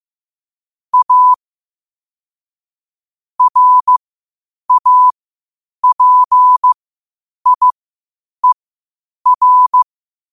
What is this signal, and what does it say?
Morse code, heard as a keyed tone, about 15 wpm, spaced as for 8 wpm, 1000 Hz. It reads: A RAPIER
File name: qotdfiles / 8wpm